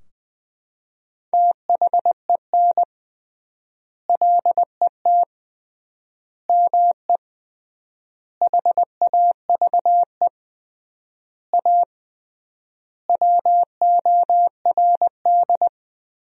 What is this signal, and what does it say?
Morse code, heard as a keyed tone, about 21 wpm, 700 Hz. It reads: THEN LET ME HAVE A WORD
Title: 3x Word-Spacing - Sentences from Top 300 Words - 20wpm